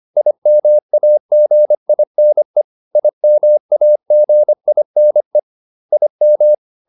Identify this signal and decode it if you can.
Morse code, heard as a keyed tone, about 25 wpm, 600 Hz. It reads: IMAGINE IMAGINE IM